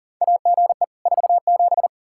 Morse code, heard as a keyed tone, about 40 wpm, 700 Hz. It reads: AGE 47
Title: Practice QSO - Exchange 3 - More Info - 40wpm